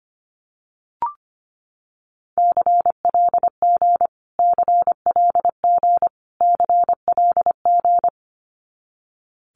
Morse code, heard as a keyed tone, about 25 wpm, 700 Hz. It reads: CLG CLG CLG
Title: Abbreviations Repeated 3x - 25wpm